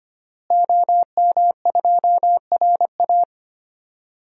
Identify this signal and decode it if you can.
Morse code, heard as a keyed tone, about 25 wpm, 700 Hz. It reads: OM2RA
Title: DXCC-100 Call Signs - 5 Characters - Repeated 2x - 25wpm